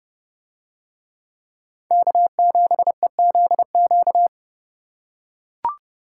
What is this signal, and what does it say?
Morse code, heard as a keyed tone, about 29 wpm, 700 Hz. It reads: K7EZQ